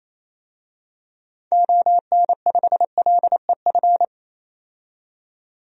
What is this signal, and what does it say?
Morse code, heard as a keyed tone, about 27 wpm, 700 Hz. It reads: ON5LEF
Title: DXCC-100 Call Signs - 6 Characters - 28wpm